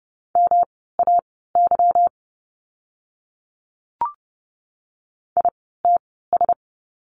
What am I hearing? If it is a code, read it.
Morse code, heard as a keyed tone, about 30 wpm, spaced as for 17 wpm, 700 Hz. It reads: MAY ITS
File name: course / 30:10wpm / 230 Lesson 37f - Letter x - Words 30wpm